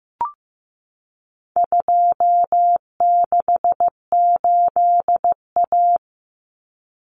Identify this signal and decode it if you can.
Morse code, heard as a keyed tone, about 15 wpm, 700 Hz. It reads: 268A
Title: Common Cut Numbers - 4-Digit 15wpm